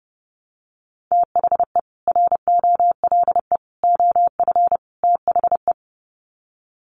Morse code, heard as a keyed tone, about 30 wpm, 700 Hz. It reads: THE ROLE OF THE
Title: Sets of 4 Words 30wpm